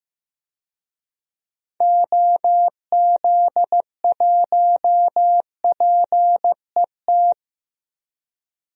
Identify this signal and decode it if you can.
Morse code, heard as a keyed tone, about 15 wpm, 700 Hz. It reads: OZ1PET
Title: DXCC-100 Call Signs - 6 Characters - 15wpm